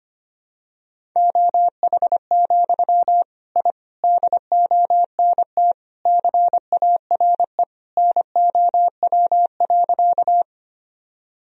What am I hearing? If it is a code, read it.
Morse code, heard as a keyed tone, about 25 wpm, 700 Hz. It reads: OH, I DONT CARE NOW.
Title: Sentences from Top 1000 Words - Review Part 1 - 25wpm